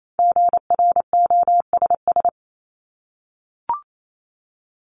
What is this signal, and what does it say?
Morse code, heard as a keyed tone, about 28 wpm, 700 Hz. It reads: GROSS
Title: Top 1751-2000 Words 28wpm